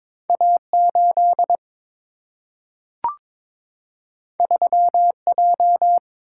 Morse code, heard as a keyed tone, about 22 wpm, 700 Hz. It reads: A8 3J
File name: practice / 22wpm / Two Letter-Number Combinations - 22wpm